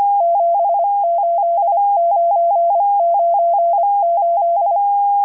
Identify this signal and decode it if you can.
Morse code, heard as a keyed tone, about 25 wpm, 680 Hz. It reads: 78998